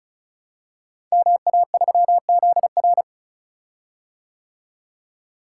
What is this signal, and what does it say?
Morse code, heard as a keyed tone, about 36 wpm, 700 Hz. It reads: MA3ZR
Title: Code Groups - 5 Letter:Numbers - 35wpm